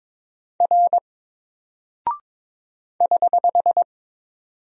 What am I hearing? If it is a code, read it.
Morse code, heard as a keyed tone, about 22 wpm, 700 Hz. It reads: R <HH>
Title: Single Character (Letter:Number:Punctuation:Prosign) 22wpm